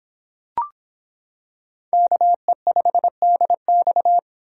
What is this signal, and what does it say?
Morse code, heard as a keyed tone, about 25 wpm, 700 Hz. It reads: KE5DX